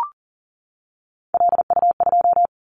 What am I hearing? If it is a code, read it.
Morse code, heard as a keyed tone, about 38 wpm, 700 Hz. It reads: LU2
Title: Callsign Prefixes for Top 100 DXCC - 40wpm